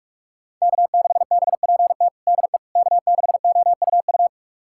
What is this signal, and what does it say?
Morse code, heard as a keyed tone, about 41 wpm, 700 Hz. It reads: K6BPT DE K6OUU
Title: USA Callsign DE Callsign - 5 Characters - 45wpm